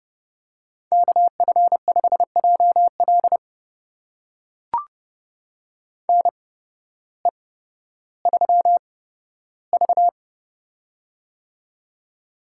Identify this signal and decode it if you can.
Morse code, heard as a keyed tone, about 30 wpm, spaced as for 14 wpm, 700 Hz. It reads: KF5JL NE3V
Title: USA Call Signs - 8x Character Spacing - 30wpm